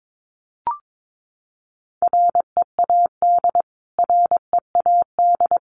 Morse code, heard as a keyed tone, about 22 wpm, 700 Hz. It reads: READ READ
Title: Top 500 Words Repeated 22wpm